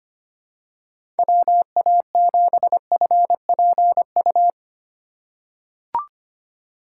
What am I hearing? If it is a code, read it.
Morse code, heard as a keyed tone, about 25 wpm, 700 Hz. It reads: WA7FPU
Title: USA Call Signs - 6 Characters - 3x Character Spacing - 25wpm